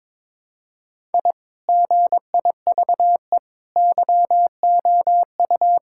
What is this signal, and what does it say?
Morse code, heard as a keyed tone, about 22 wpm, 700 Hz. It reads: I GIVE YOU
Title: Sentences from Top 1000 Words - Review Part 5 - 22wpm